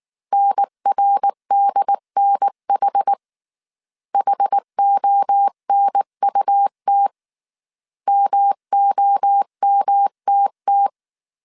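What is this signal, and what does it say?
Morse code, heard as a keyed tone, about 19 wpm, 800 Hz. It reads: NRDNH HONUT MOMTT
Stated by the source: Morse code dictation at 18 wpm from the IW3SGT QRS Morse code training course, five-character groups of 300 characters.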